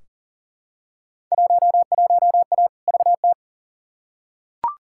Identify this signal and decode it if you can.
Morse code, heard as a keyed tone, about 40 wpm, 700 Hz. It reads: 11A VT
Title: Field Day Exchanges Encore - 40wpm